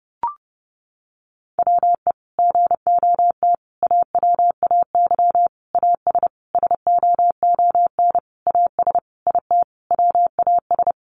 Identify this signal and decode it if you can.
Morse code, heard as a keyed tone, about 30 wpm, 700 Hz. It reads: WE GOT AWAY AS SOON AS IT WAS
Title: Sentences from Top 600 Words 30wpm